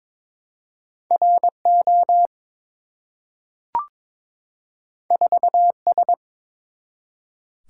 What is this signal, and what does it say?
Morse code, heard as a keyed tone, about 22 wpm, 700 Hz. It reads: RO 4S
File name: practice / 22wpm / Two Letter-Number Combinations - 22wpm